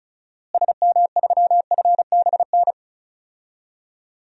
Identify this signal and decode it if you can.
Morse code, heard as a keyed tone, about 36 wpm, 700 Hz. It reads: SM3FBN
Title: DXCC-100 Call Signs - 6 Characters - Repeated 2x - 35wpm